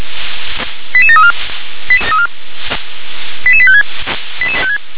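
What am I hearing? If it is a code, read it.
Five-tone selective call (ZVEI), heard as quick runs of five tones, tones of about 70 ms. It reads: 88734 88856
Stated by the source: ZVEI five-tone call received with an SDR.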